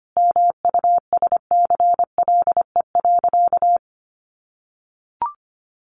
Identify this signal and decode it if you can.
Morse code, heard as a keyed tone, about 25 wpm, 700 Hz. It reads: MUSCLE.
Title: Sentences from Top 1300 Words 25wpm